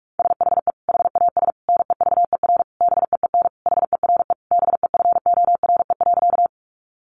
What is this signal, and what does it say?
Morse code, heard as a keyed tone, about 44 wpm, 700 Hz. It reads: SHE HAS NEVER BEEN HERE BEFORE.
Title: Additional Easy Phrases 45wpm